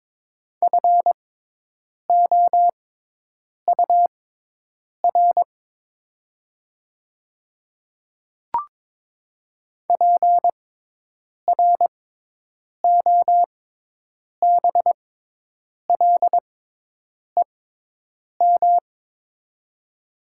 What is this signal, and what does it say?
Morse code, heard as a keyed tone, about 22 wpm, spaced as for 8 wpm, 700 Hz. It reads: FOUR PROBLEM
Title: 6x Character Spacing - Top 500 Words - 22wpm